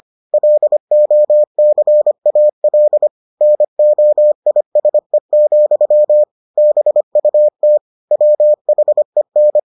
Morse code, heard as a keyed tone, about 25 wpm, 600 Hz. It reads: LOCAL NOISE, BUT WHEN